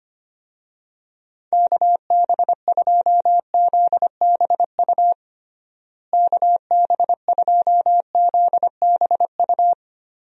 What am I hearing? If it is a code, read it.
Morse code, heard as a keyed tone, about 25 wpm, 700 Hz. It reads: KB2ZBU KB2ZBU
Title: USA Call Signs - 6 Characters - Repeated 3x - 25wpm